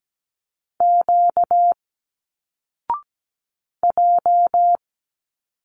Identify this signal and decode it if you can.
Morse code, heard as a keyed tone, about 17 wpm, 700 Hz. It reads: Q J
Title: Single Letters 17wpm